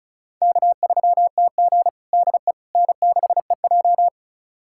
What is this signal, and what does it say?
Morse code, heard as a keyed tone, about 36 wpm, 700 Hz. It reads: K3TG DE N6EJ
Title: USA Callsign DE Callsign - 4 Characters - 35wpm